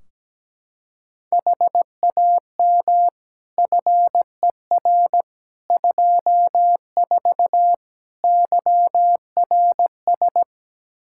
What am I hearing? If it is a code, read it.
Morse code, heard as a keyed tone, about 17 wpm, 700 Hz. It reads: HAM FER 24 YRS
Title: Practice QSO - Exchange 3 - More Info - 17wpm